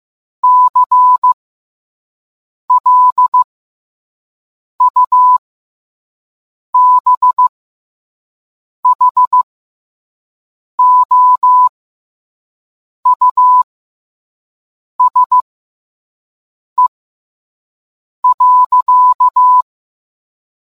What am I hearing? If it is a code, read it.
Morse code, heard as a keyed tone, about 15 wpm, spaced as for 5 wpm, 1000 Hz. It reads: CLUBHOUSE.